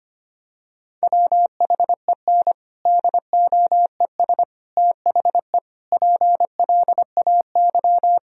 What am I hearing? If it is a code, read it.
Morse code, heard as a keyed tone, about 25 wpm, 700 Hz. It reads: WHEN DOES THE PLAY
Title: Sentences from Top 1000 Words - Review Part 1 - 25wpm